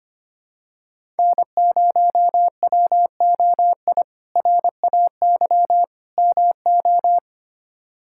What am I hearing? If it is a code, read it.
Morse code, heard as a keyed tone, about 25 wpm, 700 Hz. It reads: N0WOI RAY MO
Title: K1USN SST QSO Practice - 25wpm